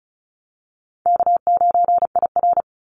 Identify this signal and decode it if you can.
Morse code, heard as a keyed tone, about 35 wpm, 700 Hz. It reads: K9IR
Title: NA SOTA Activator Callsigns 35wpm